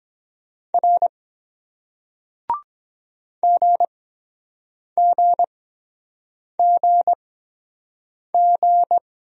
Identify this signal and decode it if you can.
Morse code, heard as a keyed tone, about 23 wpm, spaced as for 7 wpm, 700 Hz. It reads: R GGGG